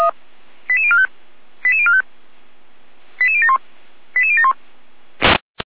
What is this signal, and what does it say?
Five-tone selective call (ZVEI), heard as quick runs of five tones, tones of about 70 ms. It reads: 88045 88045 88071 88071
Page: ZVEI